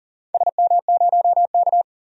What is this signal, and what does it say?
Morse code, heard as a keyed tone, about 40 wpm, 700 Hz. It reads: SM0K
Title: DXCC-100 Call Signs - 4 Characters - 40wpm